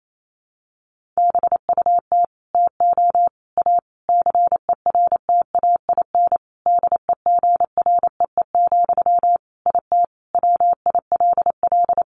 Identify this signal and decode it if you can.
Morse code, heard as a keyed tone, about 28 wpm, 700 Hz. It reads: BUT TO A CERTAIN DEGREE, IT WILL
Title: Sentences from Top 1500 Words 28wpm